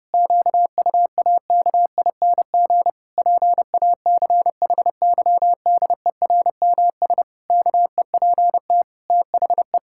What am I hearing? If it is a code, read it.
Morse code, heard as a keyed tone, about 30 wpm, 700 Hz. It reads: QUAKING PACHYDERMS KEPT THE
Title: Pangram Challenge v2 - 30wpm